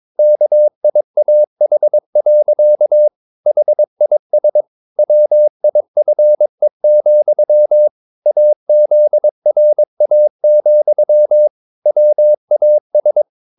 Morse code, heard as a keyed tone, about 22 wpm, 600 Hz. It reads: KIAH. HIS WIFE, AZRA, WAS
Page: Romanen Never af Ken Follet på engelsk.